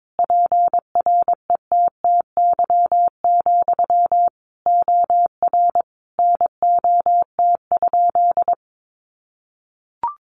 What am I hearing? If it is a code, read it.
Morse code, heard as a keyed tone, about 22 wpm, 700 Hz. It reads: PRETTY, OR NOT?
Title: Sentences from Top 1000 Words 22wpm